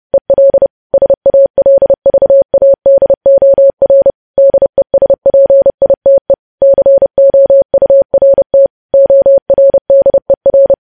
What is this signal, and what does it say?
Morse code, heard as a keyed tone, about 30 wpm, 550 Hz. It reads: EL SALVADOR DESPITE COURT ORDER
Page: News Headlines in Morse Code at 30 WPM